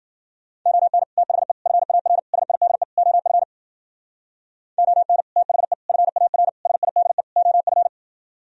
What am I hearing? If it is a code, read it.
Morse code, heard as a keyed tone, about 49 wpm, 700 Hz. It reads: ON THE FAR SIDE OF ON THE FAR SIDE OF
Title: Speed-Racing - Sets of 5 Words 40wpm